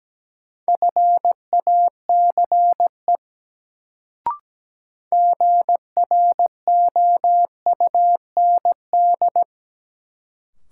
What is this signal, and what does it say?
Morse code, heard as a keyed tone, about 17 wpm, 700 Hz. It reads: FACE GROUND
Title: Sets of 3 Words - New Word Drill - 17wpm